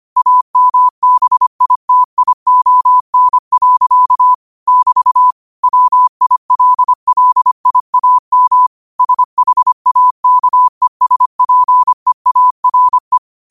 Morse code, heard as a keyed tone, about 25 wpm, 1000 Hz. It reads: AMBITION. <BT> WILLIAM SHAKESPEARE